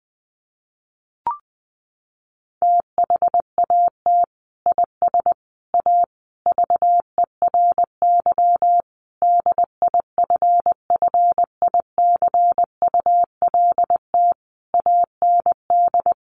Morse code, heard as a keyed tone, about 20 wpm, 700 Hz. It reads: THAT IS A VERY DIFFICULT AND
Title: Sentences from Top 1200 Words 20wpm